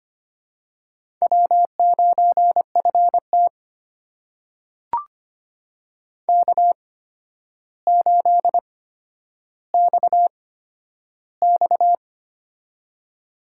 Morse code, heard as a keyed tone, about 25 wpm, spaced as for 11 wpm, 700 Hz. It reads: W9FT K8XX
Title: USA Call Signs - 4 Characters - 8x Character Spacing - 25wpm